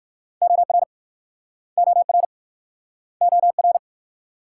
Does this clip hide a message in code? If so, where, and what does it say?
Yes: Morse code from 0.4 to 3.8 s, heard as a keyed tone, about 57 wpm, 700 Hz. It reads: OR OR OR